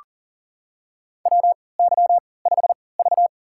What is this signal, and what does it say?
Morse code, heard as a keyed tone, about 42 wpm, spaced as for 23 wpm, 700 Hz. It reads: WY5V
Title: USA Call Signs - 4 Characters - 3x Character Spacing - 40wpm